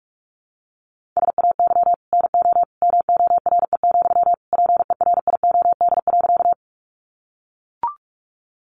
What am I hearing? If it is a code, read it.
Morse code, heard as a keyed tone, about 44 wpm, 700 Hz. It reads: SAY NO MORE, PERIOD.
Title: Sentences from Top 800 Words 45wpm